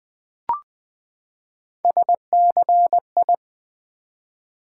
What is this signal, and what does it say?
Morse code, heard as a keyed tone, about 20 wpm, 700 Hz. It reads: SCI
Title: Build A Word - 3 Letters At A Time - 9-Letter Words - 20wpm